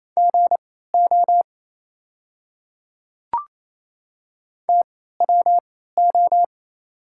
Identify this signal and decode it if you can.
Morse code, heard as a keyed tone, about 28 wpm, spaced as for 16 wpm, 700 Hz. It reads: GO TWO
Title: 3x Character Spacing - Top 200 Words - 28wpm